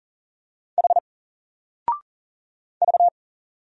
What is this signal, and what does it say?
Morse code, heard as a keyed tone, about 40 wpm, 700 Hz. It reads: H V
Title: Single Letter-Number 40wpm